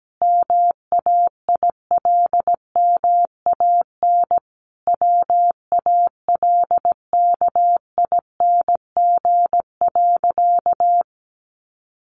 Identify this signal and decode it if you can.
Morse code, heard as a keyed tone, about 17 wpm, 700 Hz. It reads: MAILMAN WALKING.
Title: Easy Phrases 17wpm